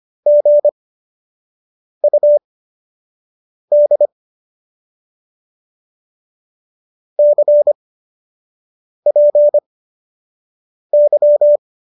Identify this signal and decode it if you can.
Morse code, heard as a keyed tone, about 25 wpm, spaced as for 6 wpm, 590 Hz. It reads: GUD CPY